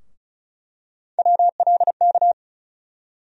Morse code, heard as a keyed tone, about 36 wpm, 700 Hz. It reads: WLK
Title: USA Call Sign Postfixes - 3 Letters 35wpm